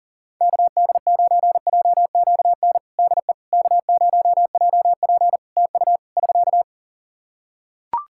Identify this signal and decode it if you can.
Morse code, heard as a keyed tone, about 40 wpm, 700 Hz. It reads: KD9JQN DE K0JP TU <SK>
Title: Practice QSO - Exchange 4 - Ending - 40wpm